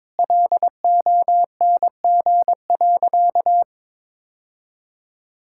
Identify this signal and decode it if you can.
Morse code, heard as a keyed tone, about 22 wpm, 700 Hz. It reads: LONG.
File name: practice / 22wpm / Sentences with 2-Word Phrases - 22wpm